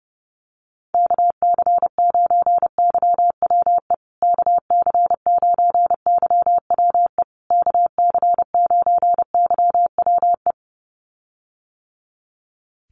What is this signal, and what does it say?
Morse code, heard as a keyed tone, about 28 wpm, 700 Hz. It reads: KC9YWE KC9YWE KC9YWE
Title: USA Call Signs Repeated 3x 30wpm